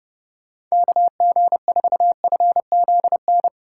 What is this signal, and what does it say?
Morse code, heard as a keyed tone, about 30 wpm, 700 Hz. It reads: KG4FZN